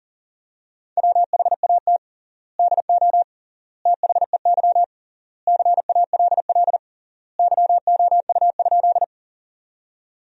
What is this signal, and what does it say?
Morse code, heard as a keyed tone, about 37 wpm, 700 Hz. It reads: WHAT DO THEY CALL YOU?
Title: 3x Word-Spacing - Sentences from Top 100 Words - 40wpm